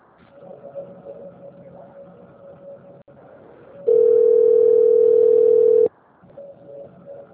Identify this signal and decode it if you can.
Ringback tone (North America), 440 + 480 Hz.